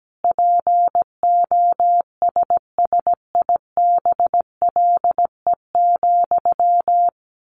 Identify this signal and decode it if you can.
Morse code, heard as a keyed tone, about 17 wpm, 700 Hz. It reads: POSSIBLE,
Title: Sentences from Top 800 Words 17wpm